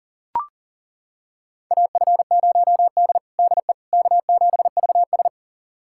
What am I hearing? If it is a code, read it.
Morse code, heard as a keyed tone, about 40 wpm, 700 Hz. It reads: AF0D DE K7VS
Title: USA Callsign DE Callsign - 4 Characters - 40wpm